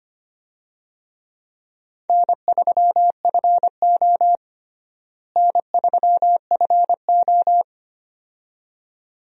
Morse code, heard as a keyed tone, about 25 wpm, 700 Hz. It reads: N3FO N3FO
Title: USA Call Signs - 4 Characters - Repeated 2x - 25wpm